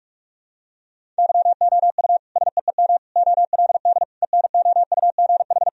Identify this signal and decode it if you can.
Morse code, heard as a keyed tone, about 42 wpm, 700 Hz. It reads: YOU SEEM OLD ENOUGH
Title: Sentences from Top 1000 Words - Review Part 1 - 45wpm